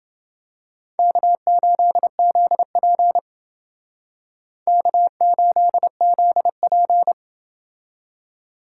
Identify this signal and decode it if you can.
Morse code, heard as a keyed tone, about 30 wpm, 700 Hz. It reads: K8ZP K8ZP